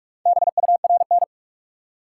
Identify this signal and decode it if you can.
Morse code, heard as a keyed tone, about 42 wpm, 700 Hz. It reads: BURN
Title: Four Letter Words - 45wpm